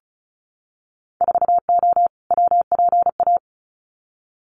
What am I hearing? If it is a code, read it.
Morse code, heard as a keyed tone, about 34 wpm, 700 Hz. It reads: 4O WPA
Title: Winter Field Day Exchanges 35wpm